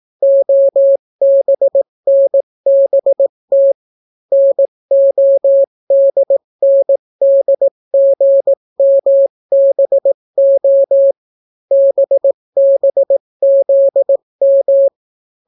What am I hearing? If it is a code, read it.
Morse code, heard as a keyed tone, about 18 wpm, 550 Hz. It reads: OBNBT NODNDGMBO BBZM